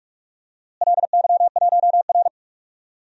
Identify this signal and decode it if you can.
Morse code, heard as a keyed tone, about 42 wpm, 700 Hz. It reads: LY1R
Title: DXCC-100 Call Signs - 4 Characters - Repeated 2x - 45wpm